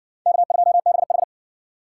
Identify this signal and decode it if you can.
Morse code, heard as a keyed tone, about 60 wpm, 700 Hz. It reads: K2BH